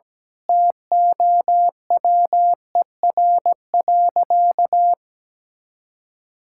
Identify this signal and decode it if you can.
Morse code, heard as a keyed tone, about 17 wpm, 700 Hz. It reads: TOWER.
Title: Pangram Challenge v2 - 17wpm